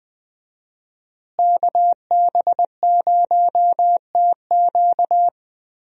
Morse code, heard as a keyed tone, about 21 wpm, 700 Hz. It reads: KB0TQ
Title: USA Call Signs - 5 Characters - Repeated 3x - 20wpm